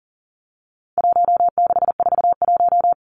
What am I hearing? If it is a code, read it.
Morse code, heard as a keyed tone, about 40 wpm, 700 Hz. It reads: 1641
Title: Numbers - Contest Style - 4-Digit 40wpm